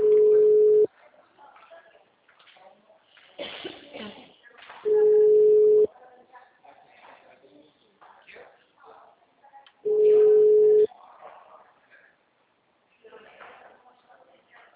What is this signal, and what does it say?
Ringback tone (Europe, 425 Hz), with speech.